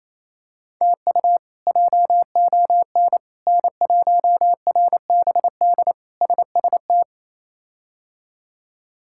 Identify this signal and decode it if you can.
Morse code, heard as a keyed tone, about 28 wpm, 700 Hz. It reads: TU JON N1RBD SST
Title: K1USN SST QSO Practice - 28wpm